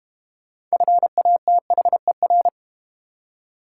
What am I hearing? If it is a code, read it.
Morse code, heard as a keyed tone, about 32 wpm, 690 Hz. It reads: FATHER